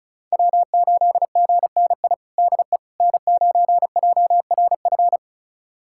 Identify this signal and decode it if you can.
Morse code, heard as a keyed tone, about 36 wpm, 700 Hz. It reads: W8GNI DE N9JRF
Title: USA Callsign DE Callsign - 5 Characters - 35wpm